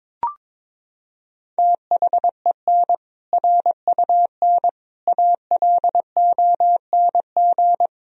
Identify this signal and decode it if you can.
Morse code, heard as a keyed tone, about 22 wpm, 700 Hz. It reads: THEN RUN ALONG
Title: Sentences from Top 1000 Words - Review Part 2 - 22wpm